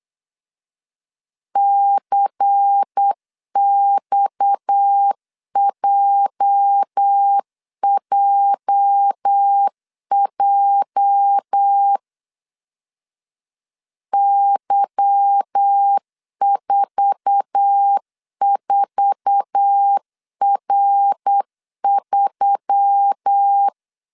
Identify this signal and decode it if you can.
Morse code, heard as a keyed tone, about 8 wpm, 800 Hz. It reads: CXJJJ Y44R3
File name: CW / DET / 1 9 1500C S B 8wpm 5624